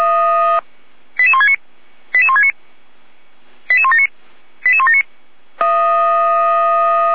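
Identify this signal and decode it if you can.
Five-tone selective call (ZVEI), heard as quick runs of five tones, tones of about 70 ms. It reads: 88169 88169 88179 88179
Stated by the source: ZVEI five-tone selective call 88105 received by SDR on channel 455, from feuerwehr-orlamuende.org fire brigade audio archive.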